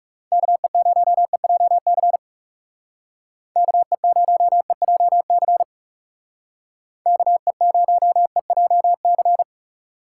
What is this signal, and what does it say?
Morse code, heard as a keyed tone, about 35 wpm, 700 Hz. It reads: KE0EJC KE0EJC KE0EJC
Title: Speed Racing - USA Call Signs 30wpm